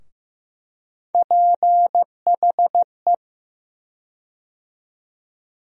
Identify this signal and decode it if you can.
Morse code, heard as a keyed tone, about 15 wpm, 700 Hz. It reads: PHE
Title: USA Call Sign Postfixes - 3 Letters 15wpm